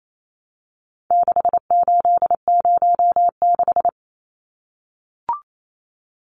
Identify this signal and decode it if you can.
Morse code, heard as a keyed tone, about 28 wpm, 700 Hz. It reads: 6806
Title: Numbers - 4-Digit 28wpm